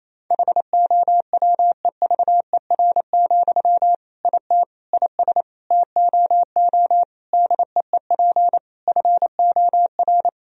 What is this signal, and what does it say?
Morse code, heard as a keyed tone, about 28 wpm, 700 Hz. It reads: HOWEVER, IT IS TOO DEEP FOR
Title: Sentences from Top 1000 Words - Review Part 3 - 28wpm